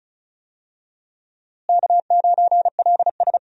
Morse code, heard as a keyed tone, about 36 wpm, 700 Hz. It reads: K9LS
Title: USA Call Signs - 4 Characters - Repeated 2x - 35wpm